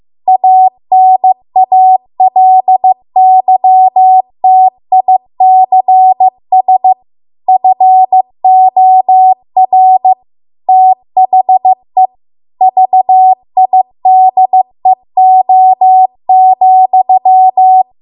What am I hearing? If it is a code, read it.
Morse code, heard as a keyed tone, about 15 wpm, 750 Hz. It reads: ANALYTICS FOR THE VIDEO,